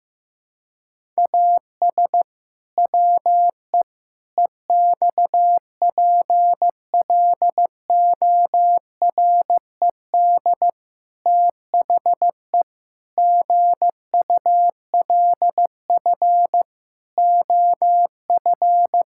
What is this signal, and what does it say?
Morse code, heard as a keyed tone, about 15 wpm, 700 Hz. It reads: AS WE EXPLORED THE GULF OF
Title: Pangram Challenge v2 - 15wpm